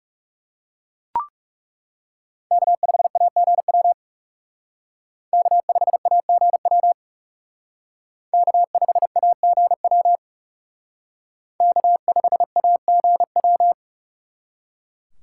Morse code, heard as a keyed tone, about 34 wpm, 690 Hz. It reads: K5AGW K5AGW K5AGW K5AGW
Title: Speed Racing - USA Call Signs 30wpm